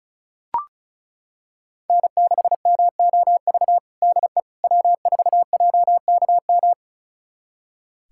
Morse code, heard as a keyed tone, about 36 wpm, 690 Hz. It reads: N6MOV DE W4JKM
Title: USA Callsign DE Callsign - 5 Characters - 35wpm